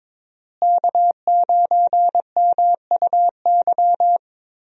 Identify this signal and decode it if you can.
Morse code, heard as a keyed tone, about 22 wpm, 700 Hz. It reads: K9MUY
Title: USA Call Signs - 5 Characters - Repeated 3x - 22wpm